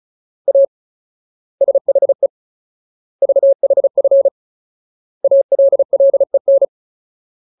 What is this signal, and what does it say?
Morse code, heard as a keyed tone, about 34 wpm, 550 Hz. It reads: A SHE VHF ALLEN
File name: Ultimate Word List / 35WPM Ultimate